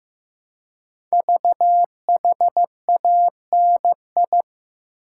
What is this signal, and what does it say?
Morse code, heard as a keyed tone, about 15 wpm, 700 Hz. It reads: VHANI
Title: Code Groups - 5 Letters - 15wpm